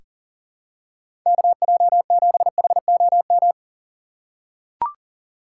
Morse code, heard as a keyed tone, about 40 wpm, 700 Hz. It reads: KJ7HOM
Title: USA Call Signs - 3x Character Spacing - 40wpm